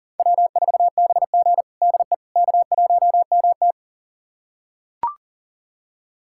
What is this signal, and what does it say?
Morse code, heard as a keyed tone, about 40 wpm, 700 Hz. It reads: W4BG DE K1MT
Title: USA Callsign DE Callsign - 4 Characters - 40wpm